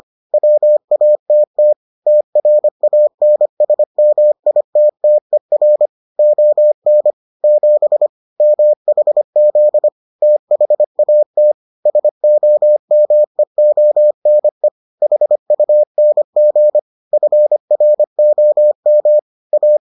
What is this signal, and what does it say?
Morse code, heard as a keyed tone, about 25 wpm, 600 Hz. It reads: WATT TRANSMITTER ON 7 MHZ THAT SOMEONE HUNG FROM A